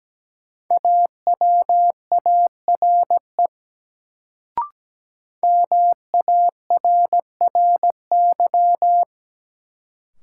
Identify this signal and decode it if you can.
Morse code, heard as a keyed tone, about 17 wpm, 700 Hz. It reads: AWARE MARRY
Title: Sentences from Top 1200 Words - New Word Drill - 17wpm